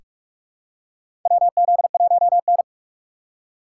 Morse code, heard as a keyed tone, about 46 wpm, 700 Hz. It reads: WZ1N
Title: USA Call Signs - 4 Characters - 6x Character Spacing - 45wpm